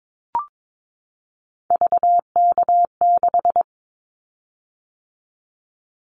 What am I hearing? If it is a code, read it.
Morse code, heard as a keyed tone, about 22 wpm, 700 Hz. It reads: VK6
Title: Callsign Prefixes for Top 100 DXCC - 22wpm